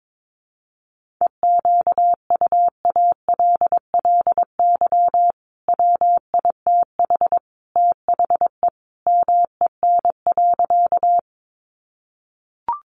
Morse code, heard as a keyed tone, about 22 wpm, 700 Hz. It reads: EQUALLY WITH THE MEN.
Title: Sentences from Top 1400 Words 22wpm